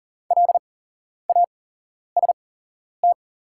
Morse code, heard as a keyed tone, about 40 wpm, spaced as for 11 wpm, 700 Hz. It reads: LAST
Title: 8x Character Spacing - Top 300 Words - 40wpm